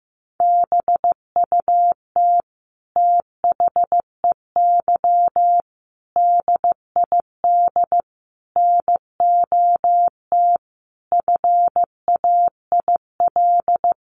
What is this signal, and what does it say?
Morse code, heard as a keyed tone, about 15 wpm, 700 Hz. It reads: BUT THEY DID NOT FAIL
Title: Sentences from Top 1100 Words 15wpm